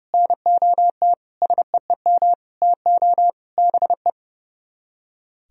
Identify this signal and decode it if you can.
Morse code, heard as a keyed tone, about 38 wpm, 700 Hz. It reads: NOT SEEM TO BE